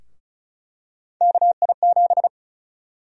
Morse code, heard as a keyed tone, about 34 wpm, 700 Hz. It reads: KI7
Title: USA Call Sign Prefixes - 35wpm